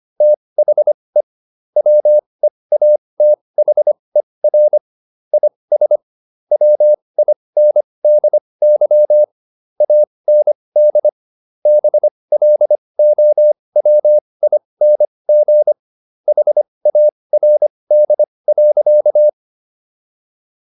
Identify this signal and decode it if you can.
Morse code, heard as a keyed tone, about 25 wpm, spaced as for 20 wpm, 600 Hz. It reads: THE WEATHER IS WINDY AND BLOWING HARD.